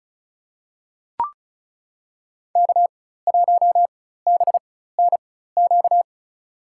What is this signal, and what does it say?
Morse code, heard as a keyed tone, about 36 wpm, spaced as for 16 wpm, 700 Hz. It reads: K1BNQ
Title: USA Call Signs - 5 Characters - 4x Character Spacing - 35wpm